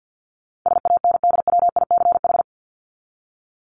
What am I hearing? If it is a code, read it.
Morse code, heard as a keyed tone, about 50 wpm, 700 Hz. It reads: SANDWICH
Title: Top 1751-2000 Words 50wpm